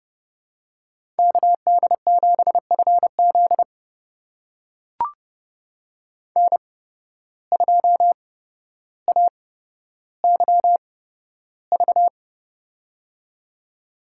Morse code, heard as a keyed tone, about 30 wpm, spaced as for 13 wpm, 700 Hz. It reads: KD7FZ N2AYV
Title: USA Call Signs - 5 Characters - 8x Character Spacing - 30wpm